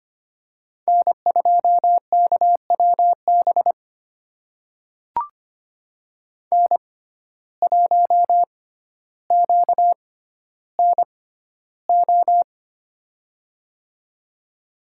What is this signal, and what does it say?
Morse code, heard as a keyed tone, about 25 wpm, spaced as for 13 wpm, 700 Hz. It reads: N2KWB N1QNO
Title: USA Call Signs - 6x Character Spacing - 25wpm